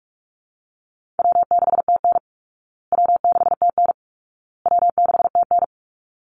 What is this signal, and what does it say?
Morse code, heard as a keyed tone, about 44 wpm, 700 Hz. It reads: W6TN W6TN W6TN
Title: Easy Call Signs Repeated 3x - 45wpm